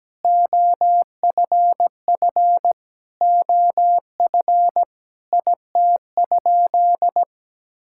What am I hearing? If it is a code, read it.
Morse code, heard as a keyed tone, about 17 wpm, 700 Hz. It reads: OFF OF IT?
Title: Sentences from Top 1000 Words - Review Part 5 - 17wpm